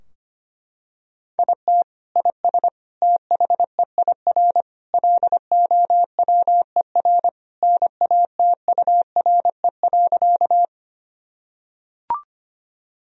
Morse code, heard as a keyed tone, about 25 wpm, 700 Hz. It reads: IT IS THEIR LOWER NATURE.
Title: Sentences from Top 1000 Words - Review Part 4 - 25wpm